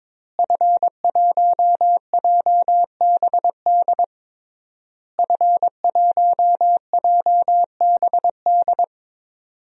Morse code, heard as a keyed tone, about 22 wpm, 700 Hz. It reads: F1JBD F1JBD
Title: DXCC-100 Call Signs - 5 Characters - Repeated 3x - 22wpm